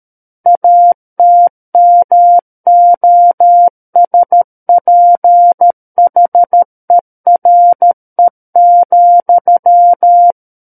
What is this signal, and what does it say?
Morse code, heard as a keyed tone, about 13 wpm, 700 Hz. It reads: ATMOSPHERE,
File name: Book One / 04-Chapter 4 - 13 WPM